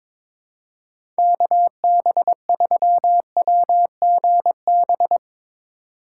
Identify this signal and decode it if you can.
Morse code, heard as a keyed tone, about 23 wpm, 700 Hz. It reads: KB3WGB
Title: USA Call Signs - 6 Characters - 8x Character Spacing - 22wpm